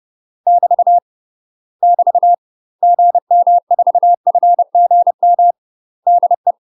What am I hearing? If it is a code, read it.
Morse code, heard as a keyed tone, about 30 wpm, 700 Hz. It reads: <BT> <BT> GM4FGM DE
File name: UK MASTER QSOs TRAINING / 30WPM / UK MASTER QSOs 0051